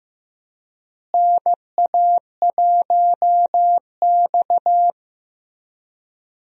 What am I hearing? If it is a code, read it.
Morse code, heard as a keyed tone, about 15 wpm, 700 Hz. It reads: NA1X
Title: USA Call Signs - 4 Characters - Repeated 2x - 15wpm